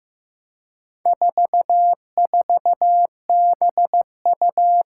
Morse code, heard as a keyed tone, about 15 wpm, 700 Hz. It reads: 44BU
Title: Four Challenging Letter-Number Combinations 15wpm